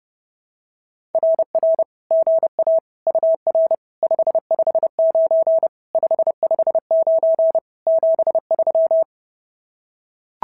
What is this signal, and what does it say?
Morse code, heard as a keyed tone, about 29 wpm, 650 Hz. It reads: RR GA UR 559 559 73
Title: SOTA QSO Practice 30wpm